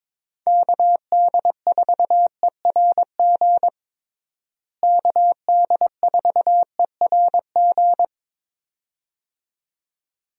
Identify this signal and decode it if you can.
Morse code, heard as a keyed tone, about 22 wpm, 700 Hz. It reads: KD4ERG KD4ERG
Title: USA Call Signs - 6 Characters - Repeated 2x - 22wpm